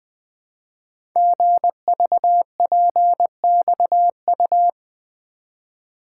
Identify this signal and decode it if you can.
Morse code, heard as a keyed tone, about 20 wpm, 700 Hz. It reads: GVPXU
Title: Code Groups - 5 Letters - 20wpm